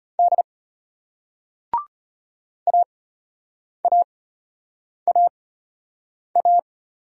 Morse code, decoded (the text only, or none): D AAAA